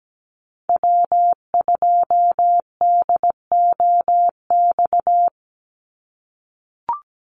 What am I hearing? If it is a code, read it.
Morse code, heard as a keyed tone, about 17 wpm, 700 Hz. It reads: W2DOX
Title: USA Call Signs 17wpm